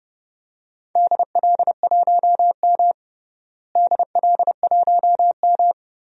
Morse code, heard as a keyed tone, about 30 wpm, 700 Hz. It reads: DL1M DL1M
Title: DXCC-100 Call Signs - 4 Characters - Repeated 2x - 30wpm